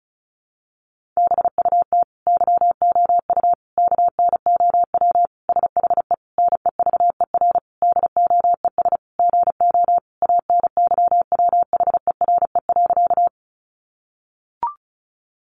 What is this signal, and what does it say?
Morse code, heard as a keyed tone, about 34 wpm, 700 Hz. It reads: BUT YOU KNOW SHE NEVER DOES GO ANYWHERE.
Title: Sentences from Top 1400 Words 35wpm